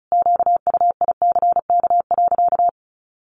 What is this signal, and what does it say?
Morse code, heard as a keyed tone, about 34 wpm, 700 Hz. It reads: QUICK.
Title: Sentences from Top 500 Words 35wpm